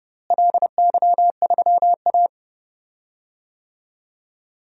Morse code, heard as a keyed tone, about 30 wpm, 700 Hz. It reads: LY3A
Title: DXCC-100 Call Signs - 4 Characters - Repeated 2x - 30wpm